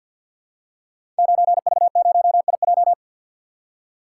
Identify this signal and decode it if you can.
Morse code, heard as a keyed tone, about 49 wpm, 700 Hz. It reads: 9V0IJ
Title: Code Groups - 5 Letter:Numbers - 50wpm